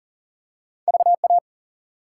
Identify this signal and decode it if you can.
Morse code, heard as a keyed tone, about 39 wpm, 700 Hz. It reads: VA
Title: US States and Canadian Provinces 40wpm